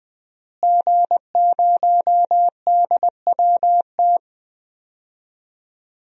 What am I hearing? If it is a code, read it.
Morse code, heard as a keyed tone, about 20 wpm, 700 Hz. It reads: G0DWT
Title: DXCC-100 Call Signs - 5 Characters - Repeated 2x - 20wpm